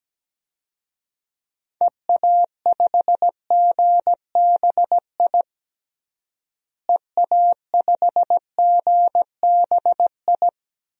Morse code, heard as a keyed tone, about 17 wpm, 700 Hz. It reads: EA5GBI EA5GBI
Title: DXCC-100 Call Signs - 6 Characters - Repeated 3x - 17wpm